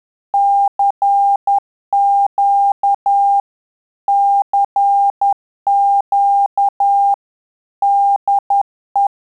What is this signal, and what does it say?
Morse code, heard as a keyed tone, about 10 wpm, 800 Hz. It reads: CQ CQ DE
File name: Telegraphy-group / QSO CW 50 CPM